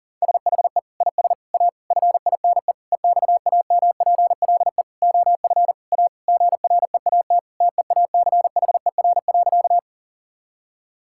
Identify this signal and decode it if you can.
Morse code, heard as a keyed tone, about 37 wpm, 700 Hz. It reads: SHE IS A FINE EXAMPLE OF A GREAT TEACHER.
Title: Sentences from Top 1000 Words - Review Part 6 - 40wpm